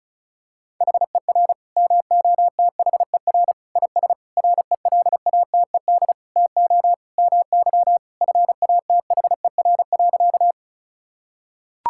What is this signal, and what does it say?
Morse code, heard as a keyed tone, about 36 wpm, 700 Hz. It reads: HER MOTHER IS RELATED TO MY FATHER.
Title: Sentences from Top 1000 Words - Review Part 5 - 35wpm